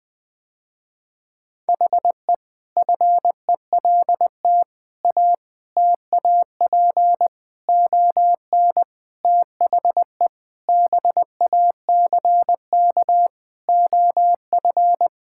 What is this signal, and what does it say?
Morse code, heard as a keyed tone, about 20 wpm, 700 Hz. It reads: HE FELT A TAP ON THE BACK OF
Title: Sentences with 4-Word Phrases - Spoken Beginning - 20wpm